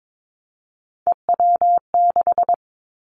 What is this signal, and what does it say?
Morse code, heard as a keyed tone, about 22 wpm, 700 Hz. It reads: EW6
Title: Callsign Prefixes for Top 50 DXCC - 22wpm